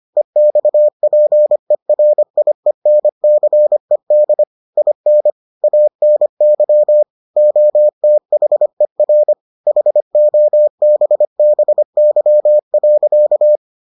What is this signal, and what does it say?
Morse code, heard as a keyed tone, about 25 wpm, 600 Hz. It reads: EXPERIENCED IN ANY OTHER HOBBY.